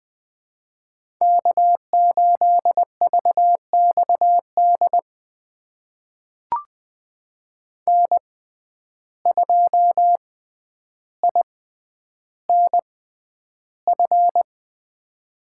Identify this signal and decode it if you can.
Morse code, heard as a keyed tone, about 20 wpm, spaced as for 10 wpm, 700 Hz. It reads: K8VXD N2INF
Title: USA Call Signs - 5 Characters - 6x Character Spacing - 20wpm